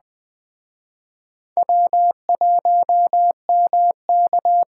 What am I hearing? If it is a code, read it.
Morse code, heard as a keyed tone, about 21 wpm, 700 Hz. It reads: W1MK